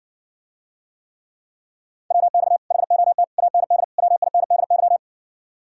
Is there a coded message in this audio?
Yes: Morse code from 2.1 to 5.0 s, heard as a keyed tone, about 60 wpm, 700 Hz. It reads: WX HOT AND WINDY